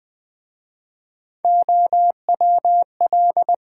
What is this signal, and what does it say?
Morse code, heard as a keyed tone, about 20 wpm, 700 Hz. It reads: OWL
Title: Easy Words with Hints - 20wpm